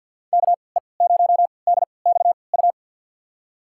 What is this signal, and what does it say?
Morse code, heard as a keyed tone, about 52 wpm, spaced as for 28 wpm, 700 Hz. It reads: KE0DXU
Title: USA Call Signs - 6 Characters - 3x Character Spacing - 50wpm